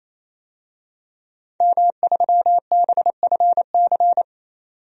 Morse code, heard as a keyed tone, about 27 wpm, 700 Hz. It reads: M3BFC
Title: DXCC-100 Call Signs - 5 Characters - 28wpm